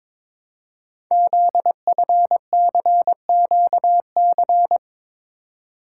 Morse code, heard as a keyed tone, about 22 wpm, 700 Hz. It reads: ZFCQC
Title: Code Groups - 5 Letters - 22wpm